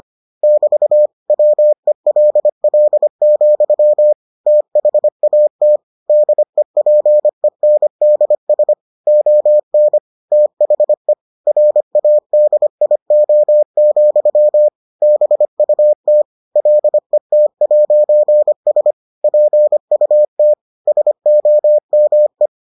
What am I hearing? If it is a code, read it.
Morse code, heard as a keyed tone, about 25 wpm, 600 Hz. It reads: <BT> WELL, THAT DEPENDS ON THE RADIO, BUT LET'S PUT SOME